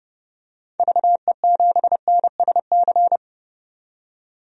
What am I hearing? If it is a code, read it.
Morse code, heard as a keyed tone, about 30 wpm, 700 Hz. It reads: VE7NSC
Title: DXCC-100 Call Signs - 6 Characters - 30wpm